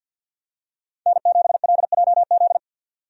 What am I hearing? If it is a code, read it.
Morse code, heard as a keyed tone, about 49 wpm, 700 Hz. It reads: N7LJZ
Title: USA Call Signs - 5 Characters - 50wpm